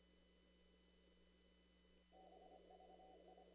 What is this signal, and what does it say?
Morse code, heard as a keyed tone, about 25 wpm, 750 Hz, with noise. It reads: BVI
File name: CW / kt5j wpx cw-op-k5pi